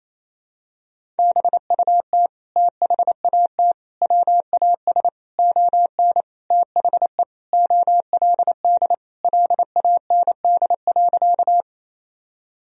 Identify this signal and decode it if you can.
Morse code, heard as a keyed tone, about 28 wpm, 700 Hz. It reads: BUT THAT WAS ON THE OLD LAND.
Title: Sentences from Top 1000 Words - Review Part 1 - 28wpm